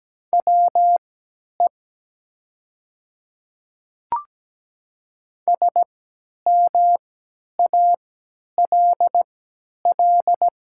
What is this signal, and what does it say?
Morse code, heard as a keyed tone, about 17 wpm, spaced as for 10 wpm, 700 Hz. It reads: WE SMALL
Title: 3x Character Spacing - Top 200 Words - 17wpm